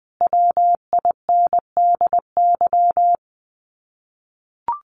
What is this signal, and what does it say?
Morse code, heard as a keyed tone, about 20 wpm, 700 Hz. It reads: WINDY
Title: Common QSO Phrases Encore 20wpm